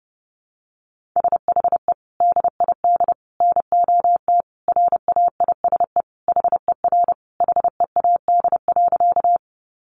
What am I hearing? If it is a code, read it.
Morse code, heard as a keyed tone, about 30 wpm, 700 Hz. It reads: SHE DID NOT RAISE HER HEAD.
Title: Sentences from Top 1400 Words 30wpm